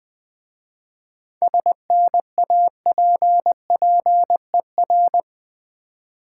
Morse code, heard as a keyed tone, about 21 wpm, 700 Hz. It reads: SNAPPER
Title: Difficult Words with Hints - 20wpm